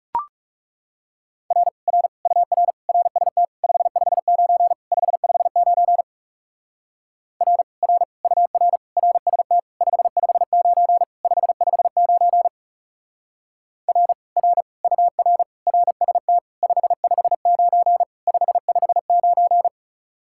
Morse code, heard as a keyed tone, about 37 wpm, 700 Hz. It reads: R R UR RST 559 559 R R UR RST 559 559 R R UR RST 559 559
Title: Speed-Racing Common QSO Phrases Encore - 30wpm